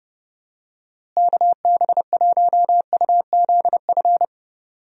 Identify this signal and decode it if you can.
Morse code, heard as a keyed tone, about 30 wpm, 700 Hz. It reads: KB1UZF
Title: USA Call Signs - 6 Characters - Repeated 2x - 30wpm